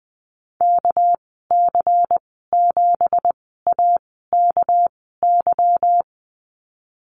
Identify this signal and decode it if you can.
Morse code, heard as a keyed tone, about 20 wpm, spaced as for 15 wpm, 700 Hz. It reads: KC7AKY
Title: 238 Lesson 38g - Letter q - Call Signs 20wpm